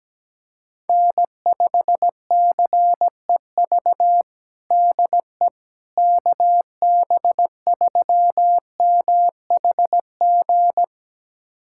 Morse code, heard as a keyed tone, about 17 wpm, 700 Hz. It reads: N5CEV DE KB3MHG
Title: USA Callsign DE Callsign 17wpm